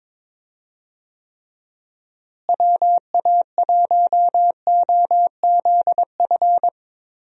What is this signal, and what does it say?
Morse code, heard as a keyed tone, about 22 wpm, 700 Hz. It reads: WA1OZF
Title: USA Call Signs - 6 Characters - Repeated 3x - 22wpm